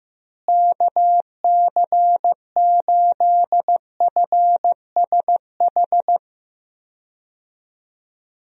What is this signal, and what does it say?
Morse code, heard as a keyed tone, about 15 wpm, 700 Hz. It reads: KC8FSH
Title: USA Call Signs - 6 Characters - Repeated 2x - 15wpm